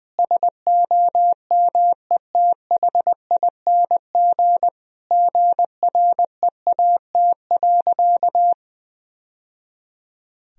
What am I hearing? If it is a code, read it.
Morse code, heard as a keyed tone, about 21 wpm, 700 Hz. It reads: SOMETHING GREAT.
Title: Sentences from Top 1000 Words - Review Part 2 - 20wpm